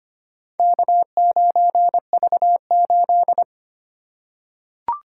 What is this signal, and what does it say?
Morse code, heard as a keyed tone, about 25 wpm, 700 Hz. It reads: K9V8
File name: practice / 25wpm / Four Letter-Number Combinations - 25wpm